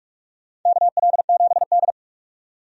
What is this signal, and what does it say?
Morse code, heard as a keyed tone, about 46 wpm, 700 Hz. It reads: KL7D
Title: DXCC-100 Call Signs - 4 Characters - 45wpm